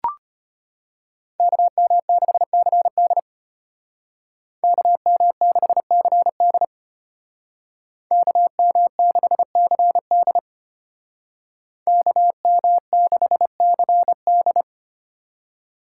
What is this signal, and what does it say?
Morse code, heard as a keyed tone, about 33 wpm, 700 Hz. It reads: KM6CD KM6CD KM6CD KM6CD